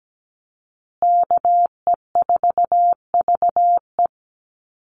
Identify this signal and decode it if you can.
Morse code, heard as a keyed tone, about 17 wpm, 700 Hz. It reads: KE4VE
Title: USA Call Signs Repeated 3x 17wpm